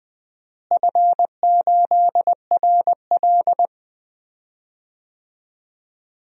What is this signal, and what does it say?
Morse code, heard as a keyed tone, about 20 wpm, 700 Hz. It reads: F8RL
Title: DXCC-100 Call Signs - 4 Characters - Repeated 2x - 20wpm